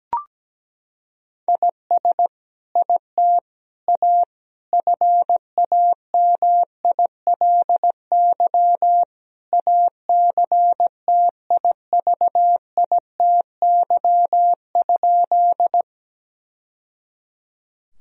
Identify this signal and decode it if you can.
Morse code, heard as a keyed tone, about 17 wpm, 700 Hz. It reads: IS IT A FAMILY ACTIVITY?
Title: Sentences from Top 1000 Words - Review Part 4 - 17wpm